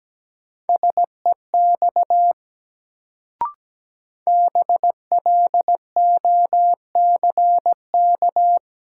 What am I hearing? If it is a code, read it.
Morse code, heard as a keyed tone, about 17 wpm, 700 Hz. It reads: SEX BLOCK
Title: Sentences from Top 1600 Words - New Word Drill - 17wpm